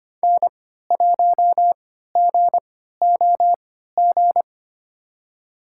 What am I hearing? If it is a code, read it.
Morse code, heard as a keyed tone, about 25 wpm, spaced as for 14 wpm, 700 Hz. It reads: N1GOG
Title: USA Call Signs - 5 Characters - 3x Character Spacing - 25wpm